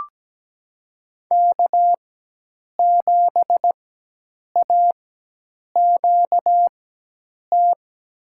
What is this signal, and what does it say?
Morse code, heard as a keyed tone, about 17 wpm, spaced as for 8 wpm, 700 Hz. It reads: K7AQT
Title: USA Call Signs - 5 Characters - 4x Character Spacing - 17wpm